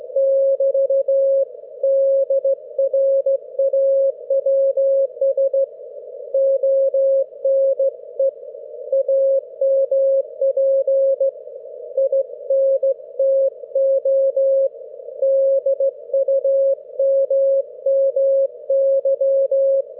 Morse code, heard as a keyed tone, about 15 wpm, 540 Hz. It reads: <BT> DRAWS ONE AMP INTO DUMMY
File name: GW3UEP / EI6DN 10am31May12